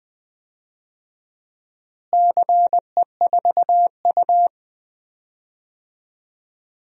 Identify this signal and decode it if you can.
Morse code, heard as a keyed tone, about 20 wpm, 700 Hz. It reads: CE4U
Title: DXCC-100 Call Signs - 4 Characters - Repeated 2x - 20wpm